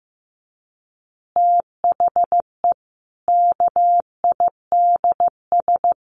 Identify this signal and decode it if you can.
Morse code, heard as a keyed tone, about 15 wpm, 700 Hz. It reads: THE KIDS
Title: Easy Phrases 15wpm